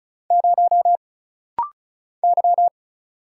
Morse code, heard as a keyed tone, about 36 wpm, 700 Hz. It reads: 0 Y
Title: Single Letter-Number Alt - Rapid-Fire 35wpm